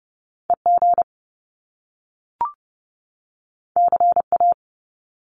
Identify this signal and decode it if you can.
Morse code, heard as a keyed tone, about 30 wpm, 700 Hz. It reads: EG CA
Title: Top Two Letter Combinations in Words 30wpm